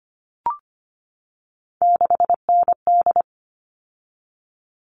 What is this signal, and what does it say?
Morse code, heard as a keyed tone, about 25 wpm, 700 Hz. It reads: 6ND
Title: All Cut Numbers - 3-Digit 25wpm